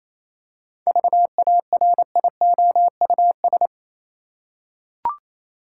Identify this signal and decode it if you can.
Morse code, heard as a keyed tone, about 28 wpm, 700 Hz. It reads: VARIOUS
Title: Seven Letter Words - 28wpm